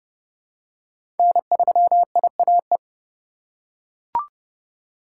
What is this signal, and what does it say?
Morse code, heard as a keyed tone, about 32 wpm, 700 Hz. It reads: N3IAE
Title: USA Call Signs - 2x Character Spacing - 30wpm